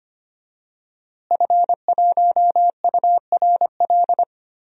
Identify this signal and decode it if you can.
Morse code, heard as a keyed tone, about 25 wpm, 700 Hz. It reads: F1URL
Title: DXCC-100 Call Signs - 5 Characters - Repeated 3x - 25wpm